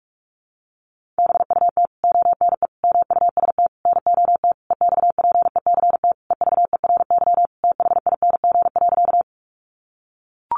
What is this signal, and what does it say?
Morse code, heard as a keyed tone, about 44 wpm, 700 Hz. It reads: BUT ONE MUST NOT EXPECT EVERYTHING.
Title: Sentences from Top 1100 Words 45wpm